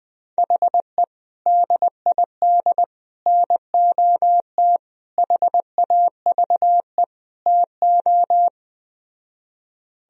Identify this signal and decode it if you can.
Morse code, heard as a keyed tone, about 20 wpm, 700 Hz. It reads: HE DID NOT HAVE TO
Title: Speed-Racing - Sets of 5 Words 17wpm